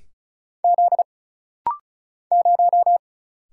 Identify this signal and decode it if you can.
Morse code, heard as a keyed tone, about 36 wpm, 700 Hz. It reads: Z 0
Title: Single Letter-Number Alt - Rapid-Fire 35wpm